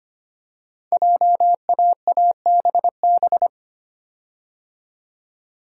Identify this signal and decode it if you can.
Morse code, heard as a keyed tone, about 25 wpm, 700 Hz. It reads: JAABB
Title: Code Groups - 5 Letters - 25wpm